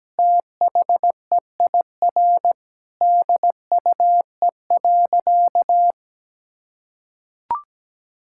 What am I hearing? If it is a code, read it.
Morse code, heard as a keyed tone, about 17 wpm, 700 Hz. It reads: THEIR DUE.
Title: Sentences from Top 1000 Words - Review Part 6 - 17wpm